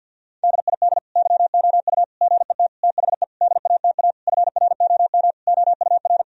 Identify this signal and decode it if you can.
Morse code, heard as a keyed tone, about 50 wpm, 700 Hz. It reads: DID YOU GET THE DATA FROM OUR
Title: Sentences from Top 1000 Words - Review Part 3 - 50wpm